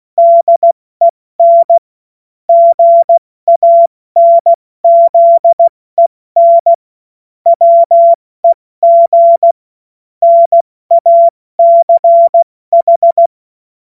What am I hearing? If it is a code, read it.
Morse code, heard as a keyed tone, about 16 wpm, 680 Hz. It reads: DEN GANZEN WEG NACH
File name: Morsekurs / Witze / Witz 06. April 2021, 70 F80